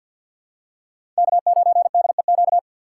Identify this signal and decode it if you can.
Morse code, heard as a keyed tone, about 53 wpm, 700 Hz. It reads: K9DEQ
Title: USA Call Signs - 5 Characters - 2x Character Spacing - 50wpm